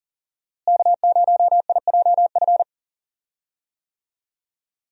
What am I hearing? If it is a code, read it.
Morse code, heard as a keyed tone, about 37 wpm, 700 Hz. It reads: K0IJF
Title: USA Call Signs - 5 Characters - Repeated 3x - 40wpm